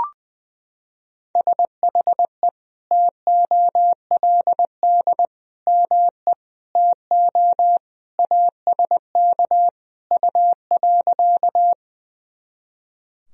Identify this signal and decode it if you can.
Morse code, heard as a keyed tone, about 20 wpm, 700 Hz. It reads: SHE TOLD ME TO ASK U.
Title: Sentences from Top 400 Words and Abbreviations 20wpm